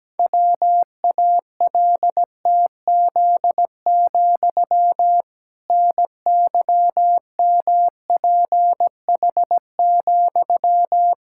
Morse code, heard as a keyed tone, about 17 wpm, 700 Hz. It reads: WALTZ, NYMPH,
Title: Pangram Challenge v2 - 17wpm